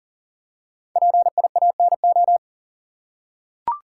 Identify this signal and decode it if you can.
Morse code, heard as a keyed tone, about 40 wpm, 700 Hz. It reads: PIANO
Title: Five Letter Words - 40wpm